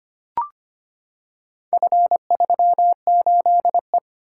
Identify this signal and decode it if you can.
Morse code, heard as a keyed tone, about 25 wpm, 700 Hz. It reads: F38E
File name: practice / 25wpm / Four Letter-Number Combinations - 25wpm